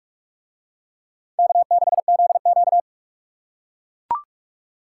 Morse code, heard as a keyed tone, about 40 wpm, 700 Hz. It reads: K6ZQ